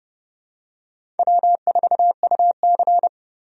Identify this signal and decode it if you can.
Morse code, heard as a keyed tone, about 29 wpm, 700 Hz. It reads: W4UC
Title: USA Call Signs - 4 Characters - 30wpm